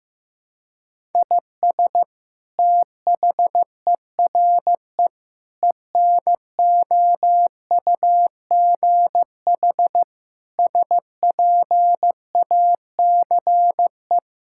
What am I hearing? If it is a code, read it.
Morse code, heard as a keyed tone, about 15 wpm, 700 Hz. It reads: IS THERE ENOUGH SPACE
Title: Sentences from Top 1000 Words - Review Part 5 - 15wpm